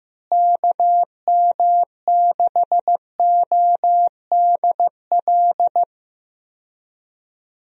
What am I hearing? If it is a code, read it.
Morse code, heard as a keyed tone, about 15 wpm, 700 Hz. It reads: KM6ODL
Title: USA Call Signs - 6 Characters - Repeated 2x - 15wpm